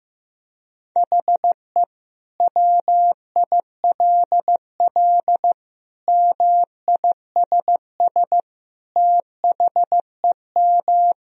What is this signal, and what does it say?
Morse code, heard as a keyed tone, about 15 wpm, 700 Hz. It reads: HE WILL MISS THEM
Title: Sentences from Top 1000 Words - Review Part 3 - 15wpm